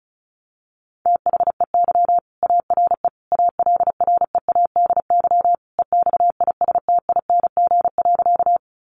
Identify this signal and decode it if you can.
Morse code, heard as a keyed tone, about 34 wpm, 700 Hz. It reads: THEY ARE ALREADY EXISTING.
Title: Sentences from Top 1500 Words 35wpm